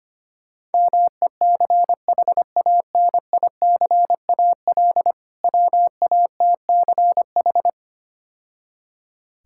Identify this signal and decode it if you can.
Morse code, heard as a keyed tone, about 25 wpm, 700 Hz. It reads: MECHANICAL WATCH
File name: practice / 25wpm / Difficult Words - Speed-Racing with Hints - 25wpm